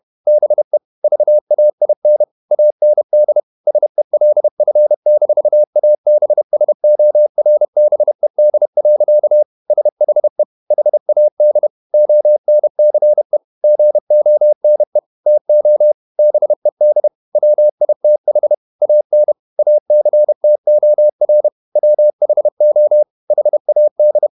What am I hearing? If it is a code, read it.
Morse code, heard as a keyed tone, about 31 wpm, 600 Hz. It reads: BE VAIN AND SELF-ABSORBED. SHE HAD ONCE GONE TO BED WITH AN ACTOR WHO HAD
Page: Never 31wpm | CW med Gnister